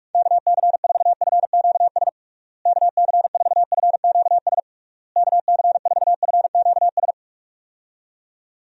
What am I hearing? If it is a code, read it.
Morse code, heard as a keyed tone, about 45 wpm, 700 Hz. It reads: KC4FQS KC4FQS KC4FQS
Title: USA Call Signs - 6 Characters - Repeated 3x - 45wpm